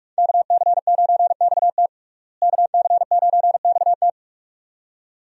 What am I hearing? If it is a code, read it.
Morse code, heard as a keyed tone, about 46 wpm, 700 Hz. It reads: KC9XT KC9XT
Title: USA Call Signs - 5 Characters - Repeated 2x - 45wpm